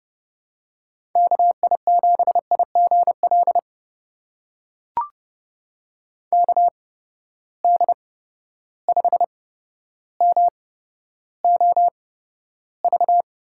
Morse code, heard as a keyed tone, about 32 wpm, spaced as for 13 wpm, 700 Hz. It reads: KI7IGL KD5MOV
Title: USA Call Signs - 6 Characters - 8x Character Spacing - 30wpm